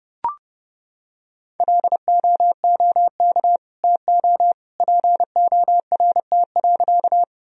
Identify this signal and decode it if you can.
Morse code, heard as a keyed tone, about 32 wpm, 700 Hz. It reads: LOOK TO PORT.
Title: Sentences from Top 1000 Words - Review Part 1 - 30wpm